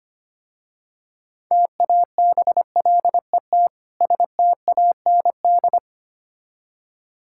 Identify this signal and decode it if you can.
Morse code, heard as a keyed tone, about 25 wpm, 700 Hz. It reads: TABLET STAND
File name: practice / 25wpm / Difficult Words with Hints - 25wpm